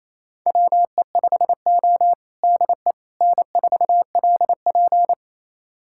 Morse code, heard as a keyed tone, about 28 wpm, 700 Hz. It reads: WE5O DE N4LP
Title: USA Callsign DE Callsign - 4 Characters - 28wpm